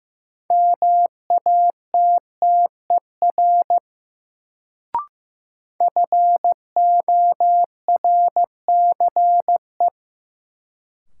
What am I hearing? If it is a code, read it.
Morse code, heard as a keyed tone, about 15 wpm, 700 Hz. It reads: MATTER FORCE
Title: Sentences from Top 700 Words - New Word Drill - 15wpm